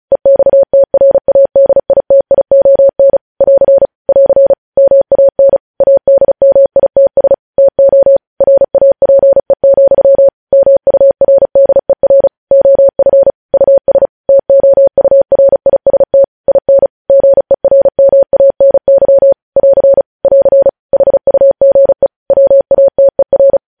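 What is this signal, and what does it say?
Morse code, heard as a keyed tone, about 36 wpm, 550 Hz. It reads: EXTRADITION <AR> <AR> MAN ADMITS TO RAPE, MURDER OF US TOURIST IN GERMANY <AR> <AR> HUGE WATER
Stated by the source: Morse code news headlines sent at 35 wpm.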